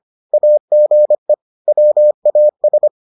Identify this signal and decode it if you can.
Morse code, heard as a keyed tone, about 25 wpm, 600 Hz. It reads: AGE WAS